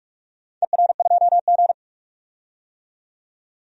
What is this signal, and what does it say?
Morse code, heard as a keyed tone, about 46 wpm, 700 Hz. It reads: ER2G
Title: DXCC-100 Call Signs - 4 Characters - 45wpm